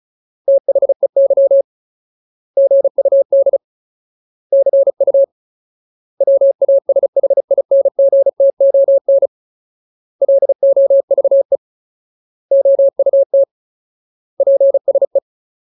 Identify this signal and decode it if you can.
Morse code, heard as a keyed tone, about 37 wpm, 550 Hz. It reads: THEY GUD CU WASHINGTON LOVE OUT PSE
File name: Ultimate Word List / 35WPM Ultimate